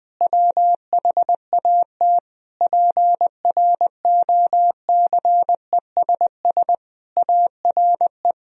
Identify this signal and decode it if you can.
Morse code, heard as a keyed tone, about 20 wpm, 700 Hz. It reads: WHAT PROCESS ARE
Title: Sentences from Top 1000 Words - Review Part 3 - 20wpm